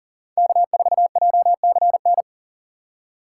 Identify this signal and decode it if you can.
Morse code, heard as a keyed tone, about 40 wpm, 700 Hz. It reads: K4JCN
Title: USA Call Signs - 5 Characters - 2x Character Spacing - 40wpm